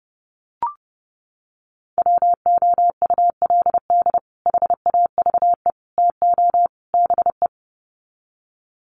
Morse code, heard as a keyed tone, about 30 wpm, 700 Hz. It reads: WOULD HAVE TO BE
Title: Sets of 4 Words 30wpm